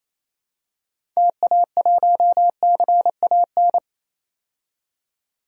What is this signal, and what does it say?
Morse code, heard as a keyed tone, about 28 wpm, 700 Hz. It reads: TA1CAN
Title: DXCC-100 Call Signs - 6 Characters - 28wpm